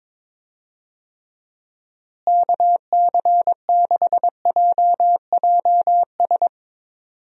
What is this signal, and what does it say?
Morse code, heard as a keyed tone, about 22 wpm, 700 Hz. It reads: KC6JJS
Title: USA Call Signs - 6 Characters - Repeated 2x - 22wpm